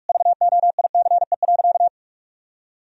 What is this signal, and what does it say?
Morse code, heard as a keyed tone, about 46 wpm, 700 Hz. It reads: VOICE.
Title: Sentences from Top 1000 Words - Review Part 1 - 45wpm